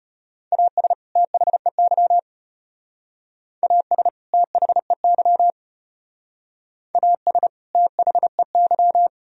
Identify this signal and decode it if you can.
Morse code, heard as a keyed tone, about 32 wpm, 700 Hz. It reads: AS THEY AS THEY AS THEY